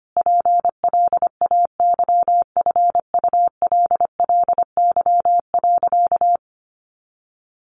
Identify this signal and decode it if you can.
Morse code, heard as a keyed tone, about 25 wpm, 700 Hz. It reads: PLAYFULLY.
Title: A Sound Of Thunder 25wpm